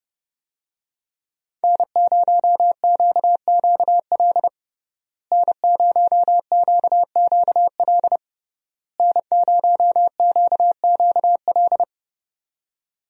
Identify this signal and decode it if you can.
Morse code, heard as a keyed tone, about 32 wpm, 700 Hz. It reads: N0QQL N0QQL N0QQL
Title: USA Call Signs - 5 Characters - Repeated 3x - 30wpm